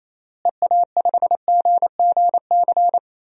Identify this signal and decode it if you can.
Morse code, heard as a keyed tone, about 27 wpm, 700 Hz. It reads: EA5GGC
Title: DXCC-100 Call Signs - 6 Characters - Repeated 2x - 28wpm